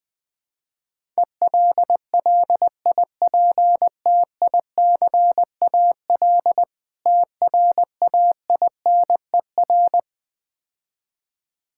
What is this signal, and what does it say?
Morse code, heard as a keyed tone, about 20 wpm, 700 Hz. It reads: ELLIPTICAL TRAINER
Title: Easy Words - Speed-Racing with Hints - 20wpm